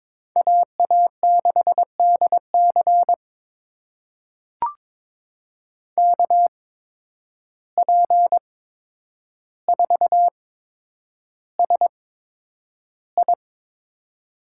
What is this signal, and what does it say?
Morse code, heard as a keyed tone, about 22 wpm, spaced as for 9 wpm, 700 Hz. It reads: AA6DC KP4SI
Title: USA Call Signs - 8x Character Spacing - 22wpm